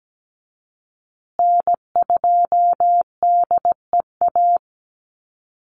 Morse code, heard as a keyed tone, about 17 wpm, 700 Hz. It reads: N2DEA
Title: USA Call Signs Repeated 3x 17wpm